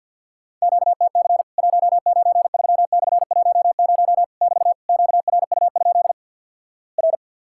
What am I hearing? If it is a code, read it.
Morse code, heard as a keyed tone, about 50 wpm, 700 Hz. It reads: QTC 193/10 <BT> QRU? R
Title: Worked All Europe DX Contest - QTC Exchange - 50wpm